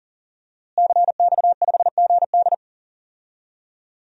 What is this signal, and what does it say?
Morse code, heard as a keyed tone, about 40 wpm, 700 Hz. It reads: CX5GD
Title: DXCC-100 Call Signs - 5 Characters - Repeated 2x - 40wpm